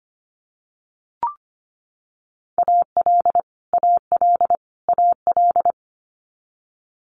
Morse code, heard as a keyed tone, about 25 wpm, 700 Hz. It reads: AL AL AL
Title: Multi-letter Phonemes Repeated 3x 25wpm